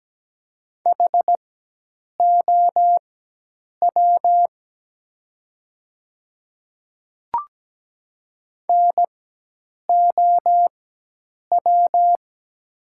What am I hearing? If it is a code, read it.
Morse code, heard as a keyed tone, about 17 wpm, spaced as for 8 wpm, 700 Hz. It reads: HOW NOW
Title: 4x Character Spacing - Top 100 Words - 17wpm